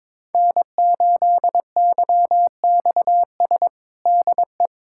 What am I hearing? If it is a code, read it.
Morse code, heard as a keyed tone, about 21 wpm, 700 Hz. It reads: N8YXS DE
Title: USA Callsign DE Callsign 22wpm